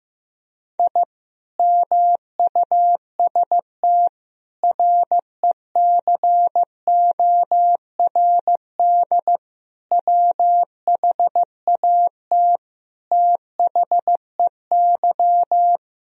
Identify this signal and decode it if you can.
Morse code, heard as a keyed tone, about 16 wpm, 700 Hz. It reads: I MUST RECORD WHAT THEY
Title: Sentences from Top 1000 Words - Review Part 3 - 15wpm